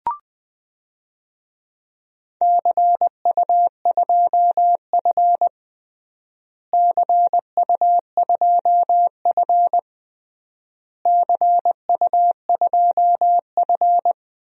Morse code, heard as a keyed tone, about 21 wpm, 700 Hz. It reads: CU2F CU2F CU2F
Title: DXCC-100 Call Signs - 4 Characters - Repeated 3x - 20wpm